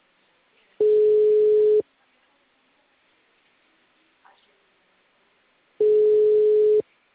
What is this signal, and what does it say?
Ringback tone (Europe, 425 Hz).